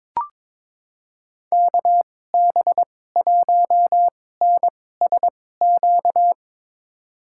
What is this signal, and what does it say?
Morse code, heard as a keyed tone, about 22 wpm, spaced as for 16 wpm, 700 Hz. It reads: KB1NSQ
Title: USA Call Signs - 6 Characters - 2x Character Spacing - 22wpm